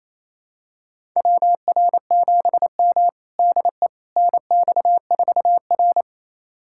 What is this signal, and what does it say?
Morse code, heard as a keyed tone, about 27 wpm, 700 Hz. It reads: WR7M DE NX4R
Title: USA Callsign DE Callsign - 4 Characters - 28wpm